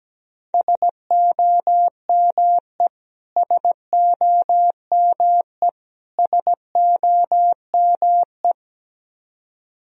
Morse code, heard as a keyed tone, about 17 wpm, 700 Hz. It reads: SOME SOME SOME
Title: Top 100 Words Repeated 17wpm